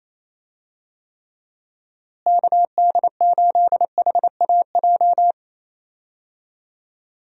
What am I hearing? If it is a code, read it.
Morse code, heard as a keyed tone, about 28 wpm, 700 Hz. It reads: KD8HAJ
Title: USA Call Signs - 6 Characters - Repeated 3x - 28wpm